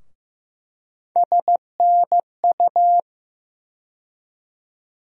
Morse code, heard as a keyed tone, about 16 wpm, 700 Hz. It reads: SNU
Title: USA Call Sign Postfixes - 3 Letters 15wpm